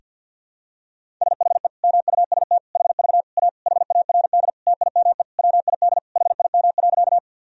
Morse code, heard as a keyed tone, about 49 wpm, 700 Hz. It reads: SHE MUST HV A HARD TIME WID HIM.
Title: Sentences from Top 300 Words and Abbreviations 50wpm